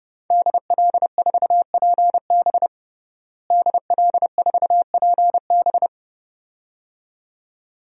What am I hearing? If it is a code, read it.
Morse code, heard as a keyed tone, about 30 wpm, 700 Hz. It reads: DL4PB DL4PB
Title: DXCC-100 Call Signs - 5 Characters - Repeated 2x - 30wpm